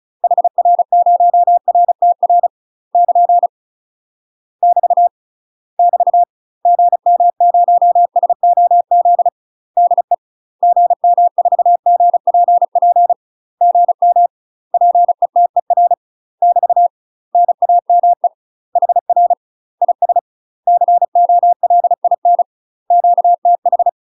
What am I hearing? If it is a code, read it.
Morse code, heard as a keyed tone, about 34 wpm, 700 Hz. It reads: HR0RTR <KN> <BT> <BT> GM0SOZ DE GM4GPP GM PETER <BT> NAME HR IS COLIN QTH
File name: UK MASTER QSOs TRAINING/35WPM